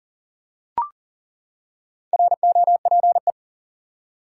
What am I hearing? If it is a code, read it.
Morse code, heard as a keyed tone, about 37 wpm, 700 Hz. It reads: ROPE
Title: Four Letter Words - 40wpm